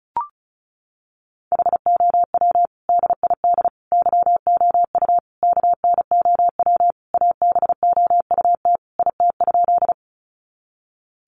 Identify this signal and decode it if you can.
Morse code, heard as a keyed tone, about 34 wpm, 700 Hz. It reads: HOW DID YOU KNOW ABOUT IT?
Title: Sentences from Top 100 Words 35wpm